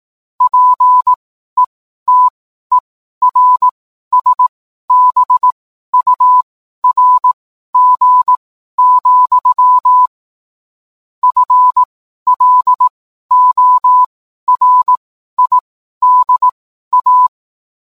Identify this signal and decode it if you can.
Morse code, heard as a keyed tone, about 18 wpm, spaced as for 12 wpm, 1000 Hz. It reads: PETERSBURG, FLORIDA